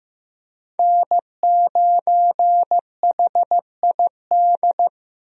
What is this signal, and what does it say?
Morse code, heard as a keyed tone, about 15 wpm, 700 Hz. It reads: N9HID
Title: USA Call Signs - 5 Characters - Repeated 2x - 15wpm